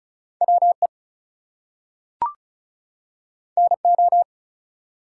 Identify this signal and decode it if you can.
Morse code, heard as a keyed tone, about 36 wpm, 700 Hz. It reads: WE NO
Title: Two Letter Words - 35wpm